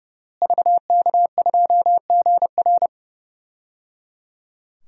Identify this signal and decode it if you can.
Morse code, heard as a keyed tone, about 30 wpm, 700 Hz. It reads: VK2GR
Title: DXCC-100 Call Signs - 5 Characters - Repeated 3x - 30wpm